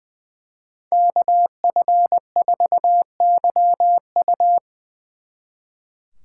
Morse code, heard as a keyed tone, about 24 wpm, 700 Hz. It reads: KF4YU
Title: Speed Racing - USA 2x2 Call Signs 20wpm